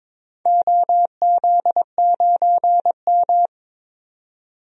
Morse code, heard as a keyed tone, about 22 wpm, 700 Hz. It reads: OZ9M
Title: DXCC-100 Call Signs - 4 Characters - 22wpm